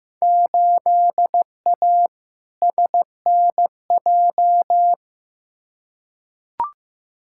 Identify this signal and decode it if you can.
Morse code, heard as a keyed tone, about 16 wpm, 700 Hz. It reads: 8A SNJ
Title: Field Day Exchanges Encore - 15wpm